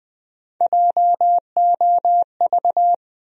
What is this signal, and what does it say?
Morse code, heard as a keyed tone, about 20 wpm, 700 Hz. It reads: JOV
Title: USA Call Sign Postfixes - 3 Letters 20wpm